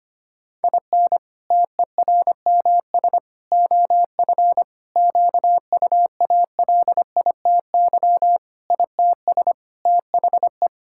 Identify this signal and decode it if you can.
Morse code, heard as a keyed tone, about 25 wpm, 700 Hz. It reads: IN TERMS OF QUALITY ITS THE
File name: practice / 25wpm / Sentences with 3-Word Phrases - 25wpm